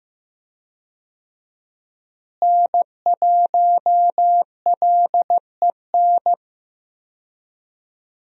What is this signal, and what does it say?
Morse code, heard as a keyed tone, about 15 wpm, 700 Hz. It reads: N1LEN
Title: USA Call Signs - 5 Characters - Repeated 3x - 15wpm